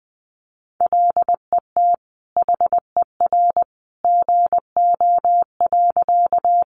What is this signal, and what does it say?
Morse code, heard as a keyed tone, about 20 wpm, 700 Hz. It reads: LET HER GO.
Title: Sentences from Top 300 Words 20wpm